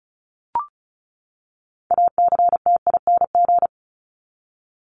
Morse code, heard as a keyed tone, about 34 wpm, 700 Hz. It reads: ACTING
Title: ING Suffix - Top 100 Words with Base Words 35wpm